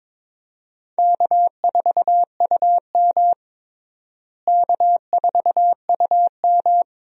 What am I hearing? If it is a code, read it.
Morse code, heard as a keyed tone, about 22 wpm, 700 Hz. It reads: K4UM K4UM
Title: USA Call Signs - 4 Characters - Repeated 2x - 22wpm